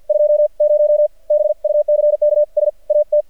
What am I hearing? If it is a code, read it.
Morse code, heard as a keyed tone, about 48 wpm, spaced as for 41 wpm, 600 Hz. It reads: 20 BUCKS AT
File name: archived / CRSnet